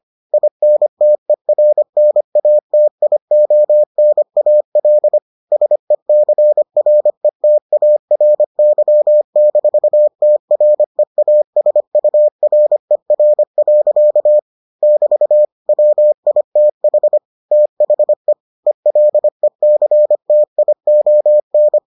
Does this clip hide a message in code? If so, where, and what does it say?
Yes: Morse code from 0.3 to 21.8 s, heard as a keyed tone, about 25 wpm, 600 Hz. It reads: INTERNATIONAL SECRETARY-TREASURER. <BT> WITH THE ELECTION